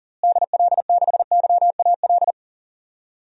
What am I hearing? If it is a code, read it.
Morse code, heard as a keyed tone, about 37 wpm, 700 Hz. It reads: DL6YAL
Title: DXCC-100 Call Signs - 6 Characters - Repeated 3x - 40wpm